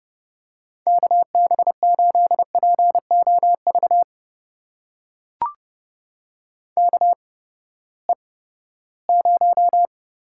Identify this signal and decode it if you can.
Morse code, heard as a keyed tone, about 30 wpm, spaced as for 17 wpm, 700 Hz. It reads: KB8POV KE0
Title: USA Call Signs - 6 Characters - 8x Character Spacing - 30wpm